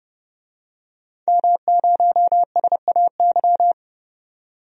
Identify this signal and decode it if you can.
Morse code, heard as a keyed tone, about 30 wpm, 700 Hz. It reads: M0SAY
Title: DXCC-100 Call Signs - 5 Characters - 30wpm